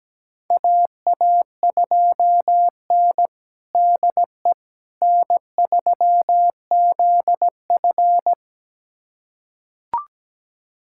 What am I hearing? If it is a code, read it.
Morse code, heard as a keyed tone, about 17 wpm, 700 Hz. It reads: AA2N DE N3ZF
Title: USA Callsign DE Callsign - 4 Characters - 17wpm